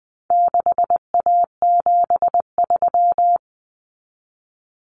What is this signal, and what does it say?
Morse code, heard as a keyed tone, about 20 wpm, 700 Hz. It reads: 6A73
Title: Common Cut Numbers - 4-Digit 20wpm